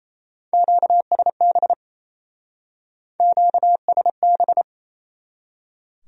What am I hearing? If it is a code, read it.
Morse code, heard as a keyed tone, about 37 wpm, 700 Hz. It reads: QSB QSB